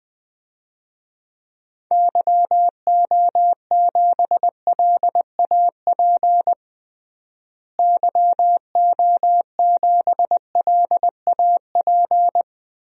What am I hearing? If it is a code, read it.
Morse code, heard as a keyed tone, about 20 wpm, 700 Hz. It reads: YO7LAP YO7LAP
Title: DXCC-100 Call Signs - 6 Characters - Repeated 3x - 20wpm